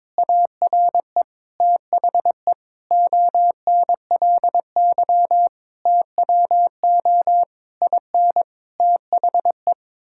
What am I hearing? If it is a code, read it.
Morse code, heard as a keyed tone, about 22 wpm, 700 Hz. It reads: ARE THE ONLY TWO IN THE
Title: Sentences from Top 1000 Words - Review Part 6 - 22wpm